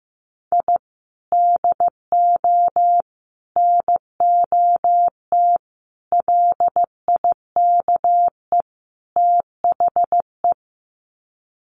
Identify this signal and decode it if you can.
Morse code, heard as a keyed tone, about 15 wpm, 700 Hz. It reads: I DO NOT LIKE THE
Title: Sets of 5 Words Encore 15wpm